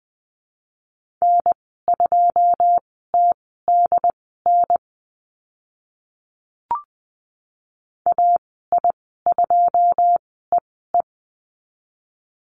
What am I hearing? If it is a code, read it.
Morse code, heard as a keyed tone, about 20 wpm, spaced as for 15 wpm, 700 Hz. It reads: N2TDN AI2EE
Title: 088 Lesson 15g - Number 5 - Call Signs 20wpm